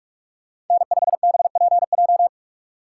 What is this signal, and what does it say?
Morse code, heard as a keyed tone, about 44 wpm, 700 Hz. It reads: N5BPJ
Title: USA Call Signs - 5 Characters - 3x Character Spacing - 45wpm